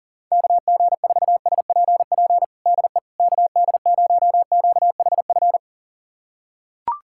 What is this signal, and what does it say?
Morse code, heard as a keyed tone, about 40 wpm, 700 Hz. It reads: KG4SPP DE KD0QHF
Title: USA Callsign DE Callsign - 6 Characters - 40wpm